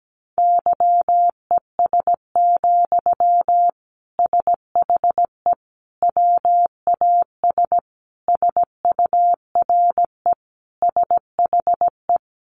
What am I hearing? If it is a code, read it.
Morse code, heard as a keyed tone, about 17 wpm, 700 Hz. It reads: YES, SHE WAS SURE SHE
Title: Sentences from Top 1100 Words 17wpm